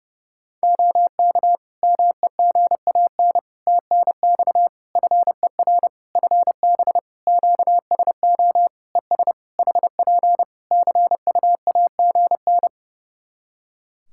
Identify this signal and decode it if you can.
Morse code, heard as a keyed tone, about 30 wpm, 700 Hz. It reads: OK MEGAN TNX FER FB QSO ES HP CUAGN
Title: Practice QSO - Exchange 4 - Ending - 30wpm